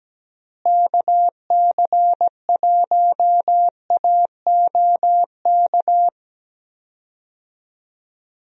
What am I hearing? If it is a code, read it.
Morse code, heard as a keyed tone, about 17 wpm, 700 Hz. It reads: KC1AOK
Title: USA Call Signs - 6 Characters - Repeated 2x - 17wpm